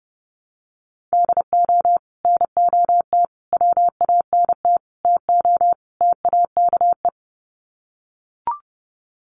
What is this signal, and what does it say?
Morse code, heard as a keyed tone, about 30 wpm, 700 Hz. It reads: DO NOT WANT TO TAKE
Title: Sets of 5 Words Encore2 30wpm